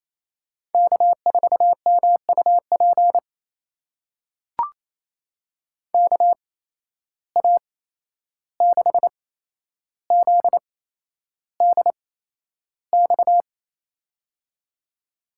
Morse code, heard as a keyed tone, about 28 wpm, spaced as for 11 wpm, 700 Hz. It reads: K4MUP KA6ZDX
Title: USA Call Signs - 8x Character Spacing - 28wpm